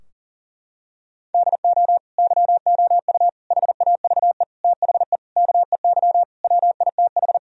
Morse code, heard as a keyed tone, about 37 wpm, 700 Hz. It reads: DO YOU HAVE THE KEY WITH
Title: Sentences from Top 1000 Words - Review Part 4 - 40wpm